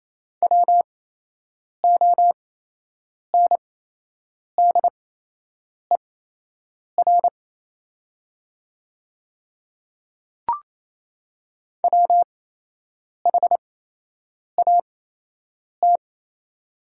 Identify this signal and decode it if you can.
Morse code, heard as a keyed tone, about 28 wpm, spaced as for 8 wpm, 700 Hz. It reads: WONDER WHAT
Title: 8x Character Spacing - Top 500 Words - 28wpm